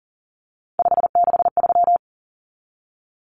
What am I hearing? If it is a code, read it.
Morse code, heard as a keyed tone, about 40 wpm, 700 Hz. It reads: 563
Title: Numbers - 3-Digit 40wpm